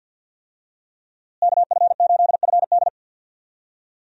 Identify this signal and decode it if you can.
Morse code, heard as a keyed tone, about 49 wpm, 700 Hz. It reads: KF8FD
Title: USA Call Signs - 5 Characters - Repeated 3x - 50wpm